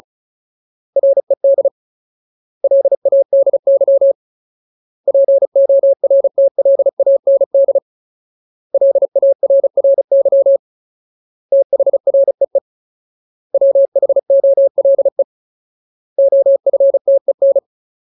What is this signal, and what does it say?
Morse code, heard as a keyed tone, about 34 wpm, 550 Hz. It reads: RED LADY PORTLAND LARRY THREE WHOLE OFTEN
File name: Ultimate Word List / 35WPM Ultimate